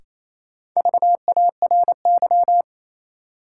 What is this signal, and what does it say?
Morse code, heard as a keyed tone, about 28 wpm, 700 Hz. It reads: VARY
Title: Sentences from Top 1200 Words - New Word Drill - 28wpm